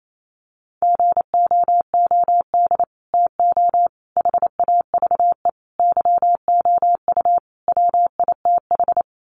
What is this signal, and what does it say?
Morse code, heard as a keyed tone, about 28 wpm, 700 Hz. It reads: GOOD TO HAVE YOU WITH
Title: Sets of 5 Words Encore2 28wpm